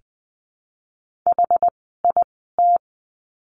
Morse code, heard as a keyed tone, about 20 wpm, spaced as for 14 wpm, 700 Hz. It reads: HIT
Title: 101 Lesson 17f - Letter u - Words 20wpm